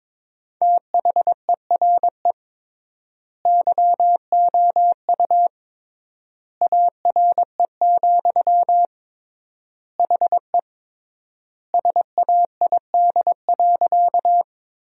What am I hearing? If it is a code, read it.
Morse code, heard as a keyed tone, about 22 wpm, 700 Hz. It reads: THERE YOU ARE, HE SAID.
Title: 3x Word-Spacing - Sentences from Top 100 Words - 22wpm